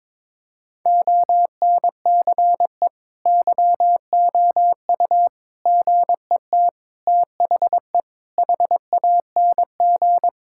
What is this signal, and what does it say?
Morse code, heard as a keyed tone, about 22 wpm, 700 Hz. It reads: ONCE YOU GET THE HANG
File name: practice / 22wpm / Sets of 7 Words - 22wpm